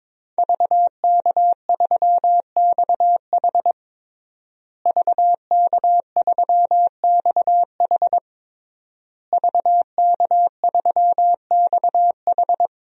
Morse code, heard as a keyed tone, about 22 wpm, 700 Hz. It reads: VK3XH VK3XH VK3XH
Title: DXCC-100 Call Signs - 5 Characters - Repeated 3x - 22wpm